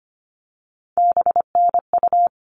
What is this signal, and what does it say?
Morse code, heard as a keyed tone, about 25 wpm, 700 Hz. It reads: BNU
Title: All Cut Numbers - 3-Digit 25wpm